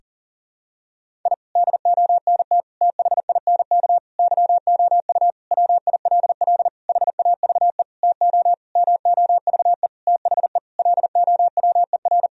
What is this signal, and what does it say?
Morse code, heard as a keyed tone, about 40 wpm, 700 Hz. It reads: I DONT THINK YOU WILL HAVE TO MOVE THE LOWER
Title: Sentences from Top 1000 Words - Review Part 4 - 40wpm